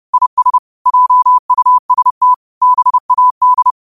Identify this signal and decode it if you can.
Morse code, heard as a keyed tone, about 30 wpm, 1000 Hz. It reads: IS JUST BAD